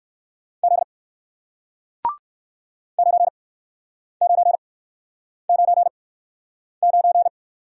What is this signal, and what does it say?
Morse code, heard as a keyed tone, about 56 wpm, spaced as for 9 wpm, 700 Hz. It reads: Z 9999
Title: Speed-Racing Single Letter-Number 45wpm